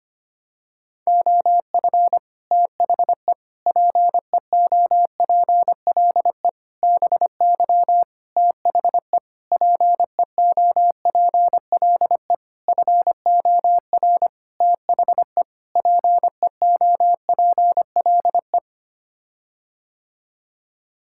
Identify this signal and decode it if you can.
Morse code, heard as a keyed tone, about 25 wpm, 700 Hz. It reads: OF THE PEOPLE BY THE PEOPLE FOR THE PEOPLE
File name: Sets of 9 Words - 25wpm